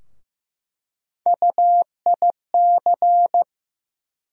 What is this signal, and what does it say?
Morse code, heard as a keyed tone, about 16 wpm, 700 Hz. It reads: UIC
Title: USA Call Sign Postfixes - 3 Letters 15wpm